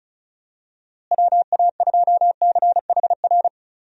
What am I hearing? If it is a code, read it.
Morse code, heard as a keyed tone, about 34 wpm, 700 Hz. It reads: WA2CHR
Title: USA Call Signs - 6 Characters - Repeated 2x - 35wpm